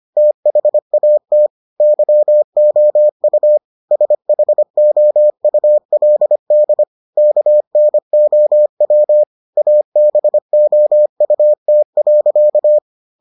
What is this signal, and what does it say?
Morse code, heard as a keyed tone, about 25 wpm, 600 Hz. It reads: THAT YOU SHOULD KNOW ABOUT.